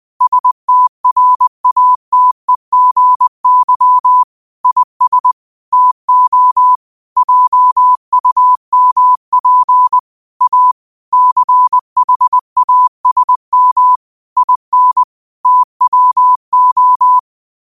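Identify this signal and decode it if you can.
Morse code, heard as a keyed tone, about 20 wpm, 1000 Hz. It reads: STRATEGY IS TO JUMP A CHASM IN TWO
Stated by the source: 20 WPM Morse code quotes, QOTD at 20 WPM.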